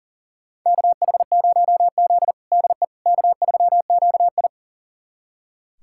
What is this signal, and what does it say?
Morse code, heard as a keyed tone, about 37 wpm, 690 Hz. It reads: KH0Z DE K3QI
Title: USA Callsign DE Callsign - 4 Characters - 40wpm